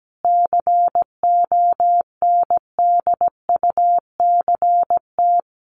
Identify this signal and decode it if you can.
Morse code, heard as a keyed tone, about 17 wpm, 700 Hz. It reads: CONDUCT
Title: A Sound Of Thunder 17wpm